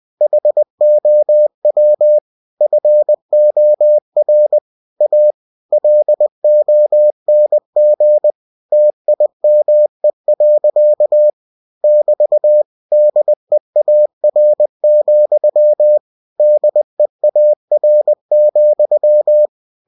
Morse code, heard as a keyed tone, about 20 wpm, 600 Hz. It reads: HOW FOR A LONG TIME. <BT> DEAR, DEAR,
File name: cw-books / winnie-the-pooh / winnie-20wpm-page-0026